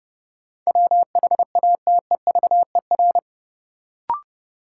Morse code, heard as a keyed tone, about 30 wpm, 700 Hz. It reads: WHATEVER
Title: Sentences from Top 1100 Words - New Word Drill - 30wpm